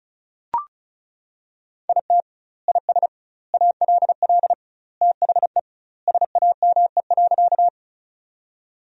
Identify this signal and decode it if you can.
Morse code, heard as a keyed tone, about 36 wpm, 700 Hz. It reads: IT IS ALL THE SAME.
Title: 2x Word-Spacing - Sentences from Top 200 Words - 35wpm